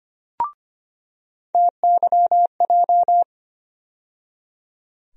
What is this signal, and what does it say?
Morse code, heard as a keyed tone, about 25 wpm, 700 Hz. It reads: TYJ
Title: USA Call Sign Postfixes - 3 Letters 25wpm